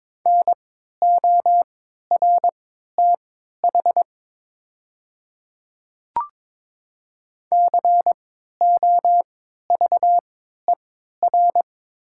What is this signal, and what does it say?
Morse code, heard as a keyed tone, about 22 wpm, spaced as for 13 wpm, 690 Hz. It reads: NORTH COVER
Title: 3x Character Spacing - Top 500 Words - 22wpm